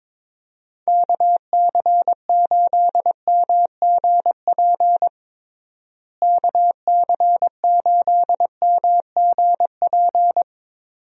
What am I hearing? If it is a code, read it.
Morse code, heard as a keyed tone, about 22 wpm, 700 Hz. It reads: KC8MGP KC8MGP
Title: USA Call Signs - 6 Characters - Repeated 2x - 22wpm